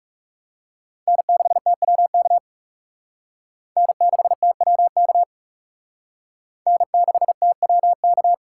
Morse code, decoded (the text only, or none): N6TWK N6TWK N6TWK